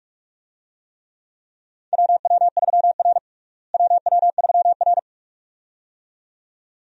Morse code, heard as a keyed tone, about 47 wpm, 700 Hz. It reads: WW3R WW3R
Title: USA Call Signs - 4 Characters - Repeated 2x - 45wpm